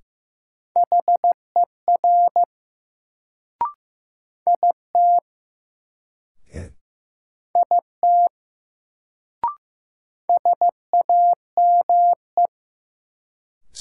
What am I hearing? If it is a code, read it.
Morse code, heard as a keyed tone, about 15 wpm, 700 Hz. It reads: HER IT IT SAME
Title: Sets of 2 Words - New Word Drill - 15wpm